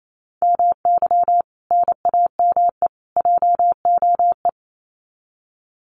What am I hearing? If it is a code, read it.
Morse code, heard as a keyed tone, about 28 wpm, 700 Hz. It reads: MY NAME JOE
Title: Common QSO Phrases 28wpm